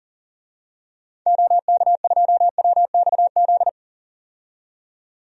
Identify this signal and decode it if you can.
Morse code, heard as a keyed tone, about 39 wpm, 700 Hz. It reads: OK2WXZ
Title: DXCC-100 Call Signs - 6 Characters - 40wpm